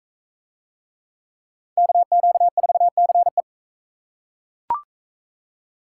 Morse code, heard as a keyed tone, about 37 wpm, 700 Hz. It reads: KQ4CE